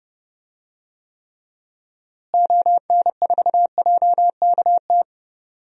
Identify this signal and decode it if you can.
Morse code, heard as a keyed tone, about 30 wpm, 700 Hz. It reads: ON4JKT
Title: DXCC-100 Call Signs - 6 Characters - Repeated 2x - 30wpm